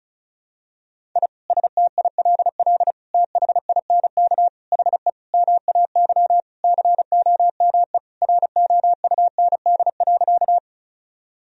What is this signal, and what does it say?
Morse code, heard as a keyed tone, about 34 wpm, 700 Hz. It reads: I STILL THINK HE MAY COME ROUND.
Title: Sentences from Top 1000 Words - Review Part 1 - 35wpm